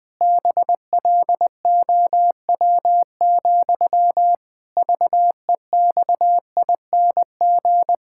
Morse code, heard as a keyed tone, about 20 wpm, 700 Hz. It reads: BLOW, VEXING
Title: Pangram Challenge v2 - 20wpm